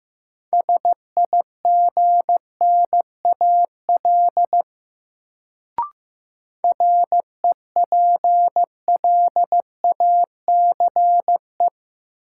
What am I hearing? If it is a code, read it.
Morse code, heard as a keyed tone, about 16 wpm, 700 Hz. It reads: SIGNAL REPLACE
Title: Sentences from Top 1400 Words - New Word Drill - 15wpm